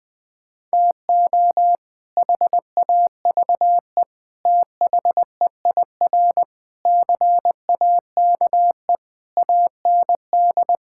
Morse code, heard as a keyed tone, about 20 wpm, 700 Hz. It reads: TO HAVE THEIR CAKE AND
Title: Sets of 8 Words - 20wpm